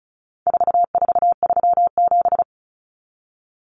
Morse code, heard as a keyed tone, about 34 wpm, 700 Hz. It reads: 4437
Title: Numbers - 4-Digit 35wpm